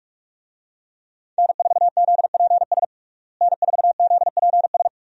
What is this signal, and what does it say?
Morse code, heard as a keyed tone, about 46 wpm, 700 Hz. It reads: N4ZPS N4ZPS
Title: USA Call Signs - 5 Characters - Repeated 3x - 45wpm